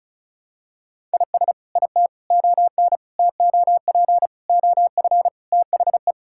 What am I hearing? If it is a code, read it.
Morse code, heard as a keyed tone, about 34 wpm, 700 Hz. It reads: IS IT ON TOP OF THE
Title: Sentences from Top 1000 Words - Review Part 6 - 35wpm